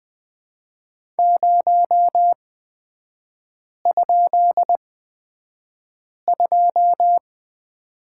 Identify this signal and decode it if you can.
Morse code, heard as a keyed tone, about 20 wpm, spaced as for 5 wpm, 700 Hz. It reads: 0?2
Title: 250 Lesson 40e - BK - Three Characters 20wpm